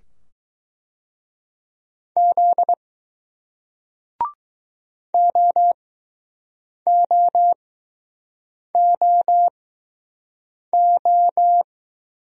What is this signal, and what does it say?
Morse code, heard as a keyed tone, about 20 wpm, spaced as for 6 wpm, 700 Hz. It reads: Z OOOO